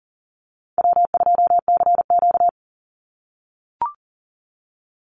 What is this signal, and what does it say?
Morse code, heard as a keyed tone, about 39 wpm, 700 Hz. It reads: W2CQ
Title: Call Signs - 40wpm